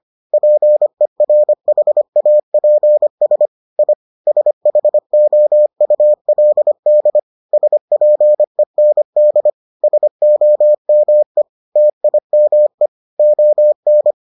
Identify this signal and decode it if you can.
Morse code, heard as a keyed tone, about 25 wpm, 600 Hz. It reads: PERHAPS I SHOULD SPEND SOME TIME ON